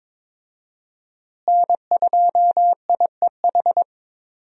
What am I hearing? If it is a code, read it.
Morse code, heard as a keyed tone, about 21 wpm, 700 Hz. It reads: N2IEH
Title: USA Call Signs - 5 Characters - Repeated 3x - 22wpm